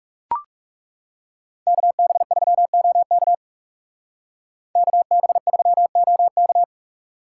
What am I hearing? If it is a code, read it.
Morse code, heard as a keyed tone, about 34 wpm, 700 Hz. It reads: KB3OK KB3OK
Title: Speed Racing - USA 2x2 Call Signs 30wpm